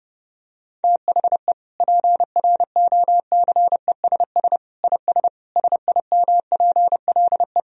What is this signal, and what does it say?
Morse code, heard as a keyed tone, about 30 wpm, 700 Hz. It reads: THE PROCESS IS SIMPLE
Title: Sentences from Top 1000 Words - Review Part 3 - 30wpm